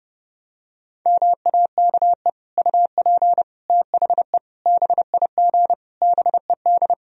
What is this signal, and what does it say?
Morse code, heard as a keyed tone, about 32 wpm, 700 Hz. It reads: MAKE UP THE BIG BED
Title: Sentences from Top 1000 Words - Review Part 1 - 30wpm